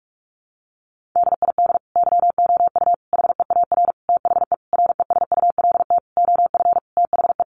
Morse code, heard as a keyed tone, about 44 wpm, 700 Hz. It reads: DID YOU HEAR THE RESULT OF THE
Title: Sentences from Top 1400 Words 45wpm